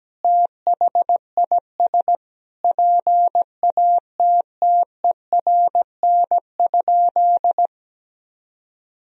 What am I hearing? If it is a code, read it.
Morse code, heard as a keyed tone, about 17 wpm, 700 Hz. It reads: THIS PATTERN?
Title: Sentences from Top 1000 Words - Review Part 2 - 17wpm